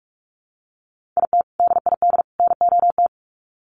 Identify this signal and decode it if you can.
Morse code, heard as a keyed tone, about 44 wpm, 700 Hz. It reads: IT DID NOT
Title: Sets of 3 Words 45wpm